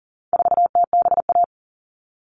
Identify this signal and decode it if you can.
Morse code, heard as a keyed tone, about 40 wpm, 700 Hz. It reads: 4TBA
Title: All Cut Numbers - 4-Digit 40wpm